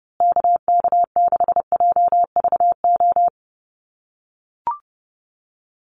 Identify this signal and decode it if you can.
Morse code, heard as a keyed tone, about 30 wpm, 700 Hz. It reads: KK6JVO
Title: USA Call Signs Repeated 3x 30wpm